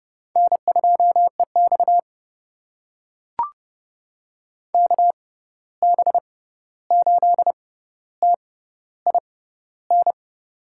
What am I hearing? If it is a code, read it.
Morse code, heard as a keyed tone, about 30 wpm, spaced as for 14 wpm, 700 Hz. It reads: N2EX KB8TIN
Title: USA Call Signs - 6x Character Spacing - 30wpm